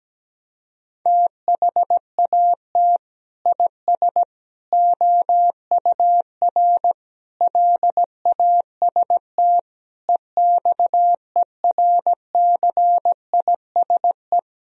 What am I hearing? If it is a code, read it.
Morse code, heard as a keyed tone, about 17 wpm, 700 Hz. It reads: THAT IS OUR LAST EXERCISE
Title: Sentences from Top 1000 Words - Review Part 4 - 17wpm